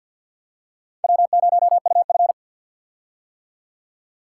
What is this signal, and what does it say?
Morse code, heard as a keyed tone, about 49 wpm, 700 Hz. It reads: W0UF
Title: K1USN SST QSO Practice - 50wpm